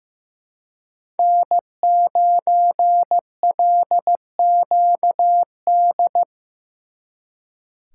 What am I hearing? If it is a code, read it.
Morse code, heard as a keyed tone, about 18 wpm, 700 Hz. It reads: N9LQD
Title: Speed Racing - USA 1x3 Call Signs 15wpm